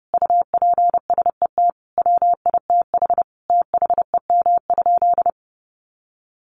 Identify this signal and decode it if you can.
Morse code, heard as a keyed tone, about 30 wpm, 700 Hz. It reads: UPSET WITH THEM?
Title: Sentences from Top 800 Words 30wpm